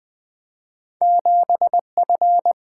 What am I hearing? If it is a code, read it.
Morse code, heard as a keyed tone, about 20 wpm, 700 Hz. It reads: 7F
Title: Two Letter-Number Combinations - 20wpm